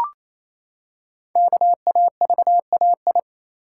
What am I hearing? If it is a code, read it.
Morse code, heard as a keyed tone, about 29 wpm, 700 Hz. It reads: KAVAI
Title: Code Groups - 5 Letters - 28wpm